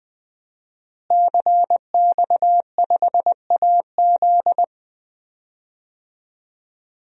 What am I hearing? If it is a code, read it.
Morse code, heard as a keyed tone, about 20 wpm, 700 Hz. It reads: CX5AZ
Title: DXCC-100 Call Signs - 5 Characters - Repeated 3x - 20wpm